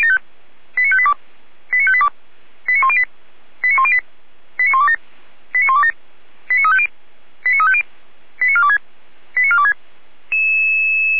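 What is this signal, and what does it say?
Five-tone selective call (ZVEI), heard as quick runs of five tones, tones of about 70 ms. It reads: 89582 89582 89108 89108 89127 89127 89350 89350 89426 89426
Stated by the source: ZVEI five-tone call received by SDR.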